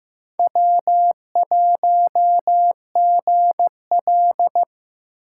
Morse code, heard as a keyed tone, about 15 wpm, 700 Hz. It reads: W1GL
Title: USA Call Signs - 4 Characters - 2x Character Spacing - 15wpm